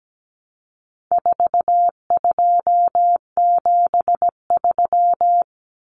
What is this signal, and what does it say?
Morse code, heard as a keyed tone, about 17 wpm, 700 Hz. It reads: 4273
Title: Numbers - 4-Digit 17wpm